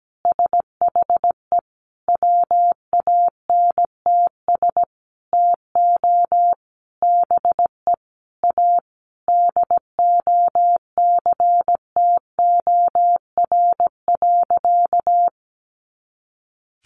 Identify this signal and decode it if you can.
Morse code, heard as a keyed tone, about 17 wpm, 700 Hz. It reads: SHE WANTS TO BE A DOCTOR.
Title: Easy Phrases 17wpm